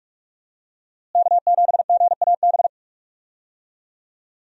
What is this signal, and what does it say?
Morse code, heard as a keyed tone, about 46 wpm, 700 Hz. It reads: K7GAB
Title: USA Call Signs - 5 Characters - Repeated 2x - 45wpm